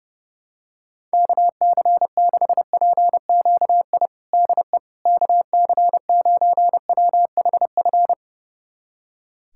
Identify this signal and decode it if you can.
Morse code, heard as a keyed tone, about 30 wpm, 700 Hz. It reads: KC6PQI DE KC9WHF
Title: USA Callsign DE Callsign - 6 Characters - 30wpm